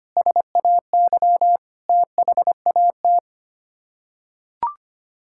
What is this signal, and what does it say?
Morse code, heard as a keyed tone, about 25 wpm, 700 Hz. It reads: SAY THAT
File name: Sets of 9 Words - 25wpm